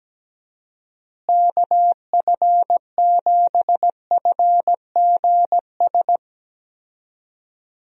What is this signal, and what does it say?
Morse code, heard as a keyed tone, about 17 wpm, 700 Hz. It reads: KF7FGS
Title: USA Call Signs - 6 Characters - Repeated 3x - 17wpm